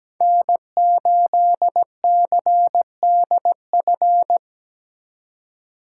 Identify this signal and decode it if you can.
Morse code, heard as a keyed tone, about 20 wpm, 700 Hz. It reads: N8CDF
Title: Speed Racing - USA 1x3 Call Signs 17wpm